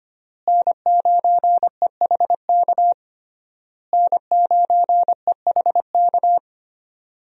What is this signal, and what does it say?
Morse code, heard as a keyed tone, about 25 wpm, 700 Hz. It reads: N9EHK N9EHK
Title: USA Call Signs - 5 Characters - Repeated 3x - 25wpm